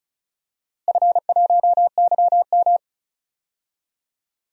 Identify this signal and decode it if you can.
Morse code, heard as a keyed tone, about 36 wpm, 700 Hz. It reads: F1YM
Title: DXCC-100 Call Signs - 4 Characters - Repeated 2x - 35wpm